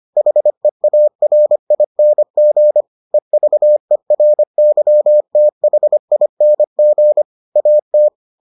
Morse code, heard as a keyed tone, about 25 wpm, 600 Hz. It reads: HEARING EVERYTHING AT